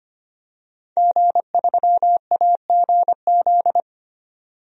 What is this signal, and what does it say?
Morse code, heard as a keyed tone, about 25 wpm, 700 Hz. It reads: G3AGZ
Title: DXCC-100 Call Signs - 5 Characters - 25wpm